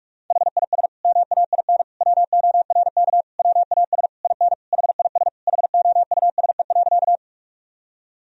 Morse code, heard as a keyed tone, about 46 wpm, 700 Hz. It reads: HIS MAIN WORK WAS IN HIS HOUSE.
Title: Sentences from Top 1000 Words - Review Part 2 - 45wpm